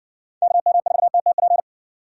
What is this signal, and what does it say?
Morse code, heard as a keyed tone, about 60 wpm, 700 Hz. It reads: KM4TTP